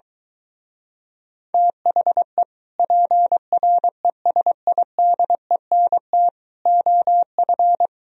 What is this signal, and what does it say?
Morse code, heard as a keyed tone, about 22 wpm, 700 Hz. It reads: THE PRESIDENT OF